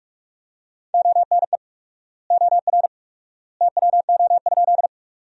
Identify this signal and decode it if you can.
Morse code, heard as a keyed tone, about 46 wpm, 700 Hz. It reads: ONE OR TWO?
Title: 4x Word-Spacing - Sentences from Top 100 Words - 45wpm